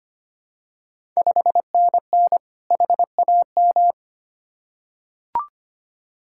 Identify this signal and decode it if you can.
Morse code, heard as a keyed tone, about 25 wpm, 700 Hz. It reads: 5NN HAM
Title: Ontario QSO Party Exchange 25wpm